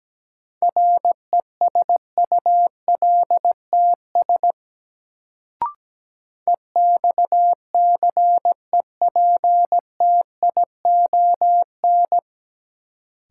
Morse code, heard as a keyed tone, about 17 wpm, 700 Hz. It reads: RESULTS EXCEPTION
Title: Sets of 4 Words - New Word Drill - 17wpm